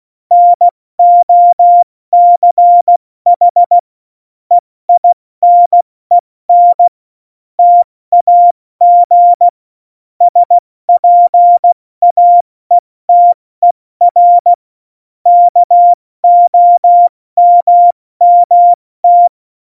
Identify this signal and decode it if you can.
Morse code, heard as a keyed tone, about 16 wpm, 700 Hz. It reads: NOCH EINEN TAG SPAETER KOMMT
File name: Morsekurs / Witze / Witz 11. September 2018, 70 F80